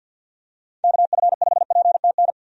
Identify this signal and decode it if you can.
Morse code, heard as a keyed tone, about 52 wpm, 700 Hz. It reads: KF5PTN
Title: USA Call Signs - 6 Characters - 50wpm